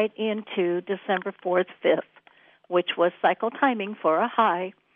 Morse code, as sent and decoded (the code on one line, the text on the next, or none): none